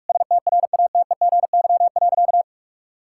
Morse code, heard as a keyed tone, about 46 wpm, 700 Hz. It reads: STRATEGY.
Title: Sentences from Top 1000 Words - Review Part 5 - 45wpm